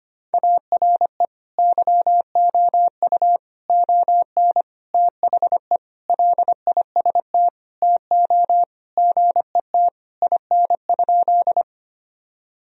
Morse code, heard as a keyed tone, about 25 wpm, 700 Hz. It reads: ARE YOU ON THE LIST TO GET IN?
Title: Sentences from Top 1000 Words - Review Part 2 - 25wpm